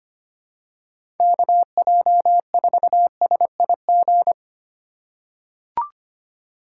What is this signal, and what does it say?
Morse code, heard as a keyed tone, about 25 wpm, 700 Hz. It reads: KJ4SIG
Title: USA Call Signs - 6 Characters - 4x Character Spacing - 25wpm